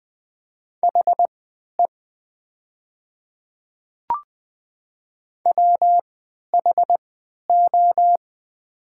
Morse code, heard as a keyed tone, about 20 wpm, spaced as for 11 wpm, 700 Hz. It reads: HE WHO
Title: 3x Character Spacing - Top 100 Words - 20wpm